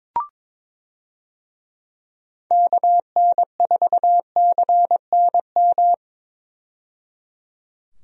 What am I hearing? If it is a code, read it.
Morse code, heard as a keyed tone, about 22 wpm, 700 Hz. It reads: KN4CNM
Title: USA Call Signs - 6 Characters - 22wpm